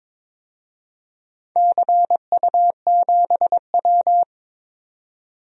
Morse code, heard as a keyed tone, about 22 wpm, 700 Hz. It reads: CU7W
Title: DXCC-100 Call Signs - 4 Characters - 22wpm